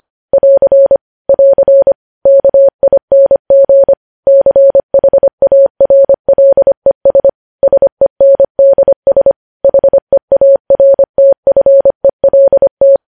Morse code, heard as a keyed tone, about 25 wpm, 550 Hz. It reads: <AR> <AR> KING CHARLES SENDS HEARTFELT
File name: podcast / News headlines-1x-25x25